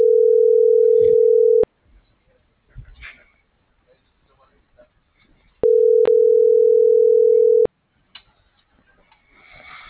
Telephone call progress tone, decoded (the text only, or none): ringback tone